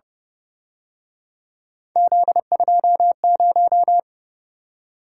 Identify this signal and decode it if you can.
Morse code, heard as a keyed tone, about 30 wpm, 700 Hz. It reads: Z20